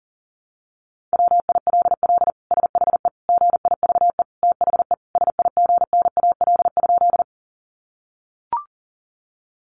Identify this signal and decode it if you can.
Morse code, heard as a keyed tone, about 40 wpm, 700 Hz. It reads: WILL SHE GIVE THE SIGNAL?
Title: Sentences from Top 1400 Words 40wpm